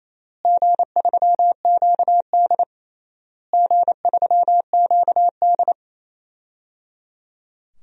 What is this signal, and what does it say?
Morse code, heard as a keyed tone, about 28 wpm, 700 Hz. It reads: G3QD G3QD
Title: DXCC-100 Call Signs - 4 Characters - Repeated 3x - 28wpm